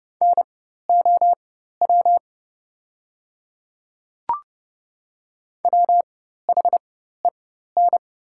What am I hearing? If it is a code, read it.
Morse code, heard as a keyed tone, about 30 wpm, spaced as for 14 wpm, 700 Hz. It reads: NOW WHEN
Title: 4x Character Spacing - Top 100 Words - 30wpm